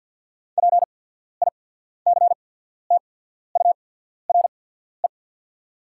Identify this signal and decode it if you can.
Morse code, heard as a keyed tone, about 48 wpm, spaced as for 14 wpm, 700 Hz. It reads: PICTURE
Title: 8x Character Spacing - Top 200 Words - 50wpm